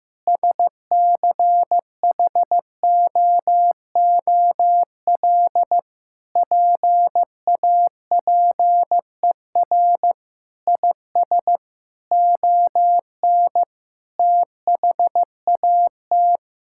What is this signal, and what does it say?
Morse code, heard as a keyed tone, about 15 wpm, 700 Hz. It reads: SCHOOL PAPER IS ON THAT
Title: Sentences from Top 1000 Words - Review Part 6 - 15wpm